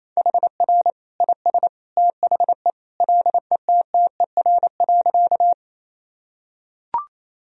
Morse code, heard as a keyed tone, about 27 wpm, 700 Hz. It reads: HR IS THE LETTER.
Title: Sentences from Top 300 Words and Abbreviations 28wpm